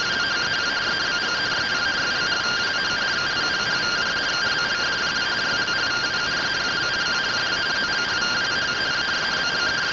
Two-tone teleprinter signal, about 170 Hz shift.
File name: findings / SAGITTAL 43mm 4mm.gen-06 rtty rx